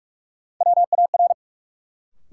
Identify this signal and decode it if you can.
Morse code, heard as a keyed tone, about 45 wpm, 700 Hz. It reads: WAR
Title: Sentences from Top 400 Words - New Word Drill - 45wpm